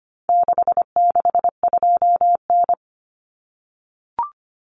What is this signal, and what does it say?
Morse code, heard as a keyed tone, about 25 wpm, 700 Hz. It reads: 662N
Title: Common Cut Numbers - 4-Digit 25wpm